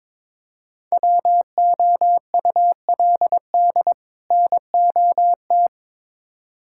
Morse code, heard as a keyed tone, about 22 wpm, 700 Hz. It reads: WOULD NOT
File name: Build A Phrase - 2-Word Phrases - 22wpm